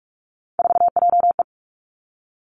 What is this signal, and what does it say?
Morse code, heard as a keyed tone, about 46 wpm, 700 Hz. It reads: 4JE
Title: Three Challenging Letter-Number Combinations 45wpm